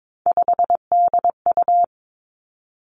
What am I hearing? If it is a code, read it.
Morse code, heard as a keyed tone, about 22 wpm, 700 Hz. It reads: 5DU
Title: Three Challenging Letter-Number Combinations 22wpm